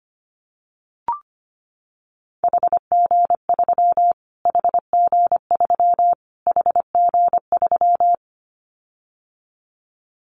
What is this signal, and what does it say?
Morse code, heard as a keyed tone, about 25 wpm, 700 Hz. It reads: HG3 HG3 HG3
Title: Callsign Prefixes for Top 25 DXCC - Repeated 3x - 25wpm